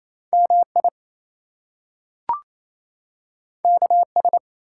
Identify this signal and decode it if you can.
Morse code, heard as a keyed tone, about 28 wpm, 700 Hz. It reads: MI KS
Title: US States and Canadian Provinces 28wpm